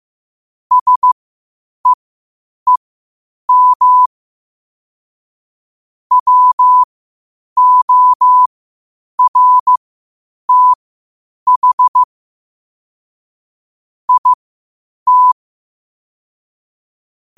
Morse code, heard as a keyed tone, about 15 wpm, spaced as for 8 wpm, 1000 Hz. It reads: SEEM WORTH IT
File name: qotdfiles / 8wpm